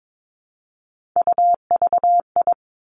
Morse code, heard as a keyed tone, about 22 wpm, 700 Hz. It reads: UVI
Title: Three Challenging Letter-Number Combinations 22wpm